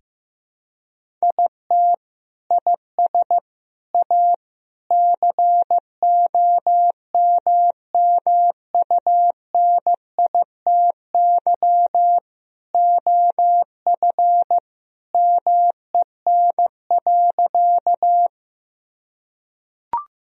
Morse code, heard as a keyed tone, about 15 wpm, 700 Hz. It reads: IT IS A COMMUNITY OF MEN.
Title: Sentences from Top 1000 Words - Review Part 4 - 15wpm